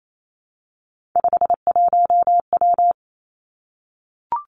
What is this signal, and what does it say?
Morse code, heard as a keyed tone, about 28 wpm, 700 Hz. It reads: 51W
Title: Three Challenging Letter-Number Combinations 28wpm